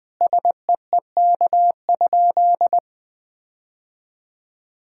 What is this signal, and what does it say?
Morse code, heard as a keyed tone, about 20 wpm, 700 Hz. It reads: SEEK?
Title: Sentences from Top 1600 Words 20wpm